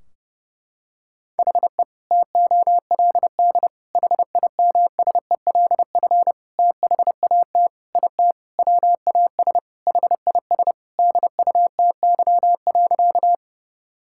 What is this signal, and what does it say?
Morse code, heard as a keyed tone, about 30 wpm, 700 Hz. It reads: HE TOLD HIMSELF THAT IT WAS HIS DUTY.
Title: Sentences from Top 1600 Words 30wpm